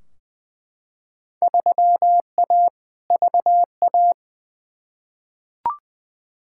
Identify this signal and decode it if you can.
Morse code, heard as a keyed tone, about 20 wpm, 700 Hz. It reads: 3A VA
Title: Field Day Exchanges - 20wpm